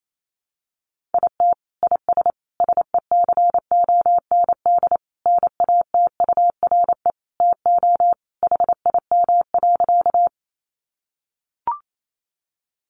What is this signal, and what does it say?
Morse code, heard as a keyed tone, about 28 wpm, 700 Hz. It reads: IT IS SECOND NATURE TO HIM.
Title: Sentences from Top 600 Words 28wpm